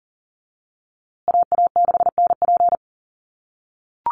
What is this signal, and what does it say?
Morse code, heard as a keyed tone, about 40 wpm, 700 Hz. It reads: AA6NP
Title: USA Call Signs Repeated 3x 40wpm